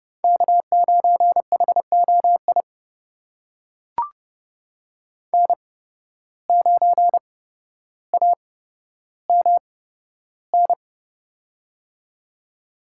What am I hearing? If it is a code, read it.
Morse code, heard as a keyed tone, about 30 wpm, spaced as for 13 wpm, 700 Hz. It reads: K9HOI N9AMN
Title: USA Call Signs - 8x Character Spacing - 30wpm